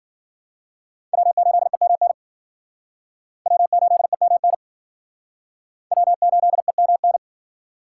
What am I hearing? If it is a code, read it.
Morse code, heard as a keyed tone, about 50 wpm, 700 Hz. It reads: W8EMN W8EMN W8EMN